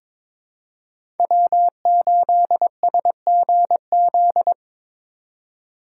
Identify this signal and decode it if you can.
Morse code, heard as a keyed tone, about 22 wpm, 700 Hz. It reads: W8SGZ
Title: USA Call Signs - 5 Characters - 3x Character Spacing - 22wpm